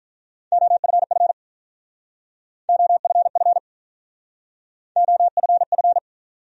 Morse code, heard as a keyed tone, about 47 wpm, 700 Hz. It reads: OFF OFF OFF